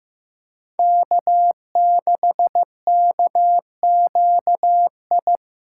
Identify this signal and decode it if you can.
Morse code, heard as a keyed tone, about 15 wpm, 700 Hz. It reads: K6KQI
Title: USA Callsign DE Callsign - 5 Characters - 15wpm